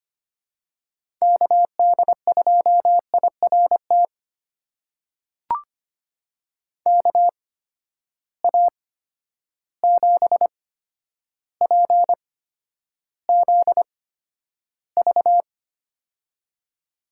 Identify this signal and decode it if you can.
Morse code, heard as a keyed tone, about 25 wpm, spaced as for 11 wpm, 700 Hz. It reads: KD2IRT KA7PZV
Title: USA Call Signs - 6 Characters - 8x Character Spacing - 25wpm